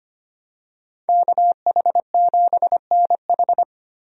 Morse code, heard as a keyed tone, about 25 wpm, 700 Hz. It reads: KH7NH
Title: USA Call Signs - 5 Characters - 25wpm